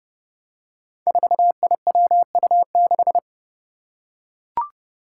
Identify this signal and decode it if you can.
Morse code, heard as a keyed tone, about 30 wpm, 700 Hz. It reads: 4IWU6
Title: Five Challenging Letter-Number Combinations 30wpm